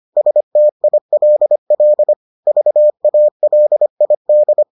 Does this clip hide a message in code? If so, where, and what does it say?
Yes: Morse code from 0.2 to 4.6 s, heard as a keyed tone, about 25 wpm, 600 Hz. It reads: STILL VALID